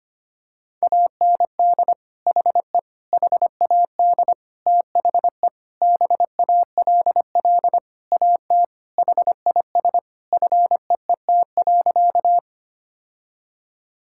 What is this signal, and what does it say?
Morse code, heard as a keyed tone, about 25 wpm, 700 Hz. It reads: AND HE HAD THE BALL AT HIS FEET.
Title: Sentences from Top 1000 Words - Review Part 6 - 25wpm